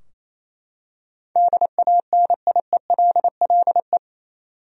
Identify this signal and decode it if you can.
Morse code, heard as a keyed tone, about 28 wpm, 700 Hz. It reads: DANIELLE
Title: USA Common Names - 28wpm